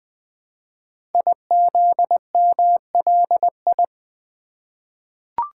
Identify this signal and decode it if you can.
Morse code, heard as a keyed tone, about 20 wpm, 700 Hz. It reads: IZMLI
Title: Code Groups - 5 Letters - 20wpm